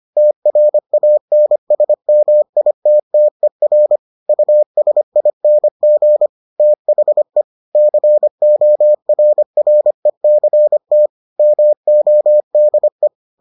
Morse code, heard as a keyed tone, about 25 wpm, 600 Hz. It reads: TRANSMITTER USING THE CORRECT MODE